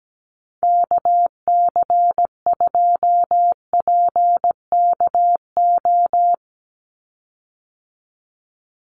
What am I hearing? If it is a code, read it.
Morse code, heard as a keyed tone, about 17 wpm, 700 Hz. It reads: KC2PKO
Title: USA Call Signs 17wpm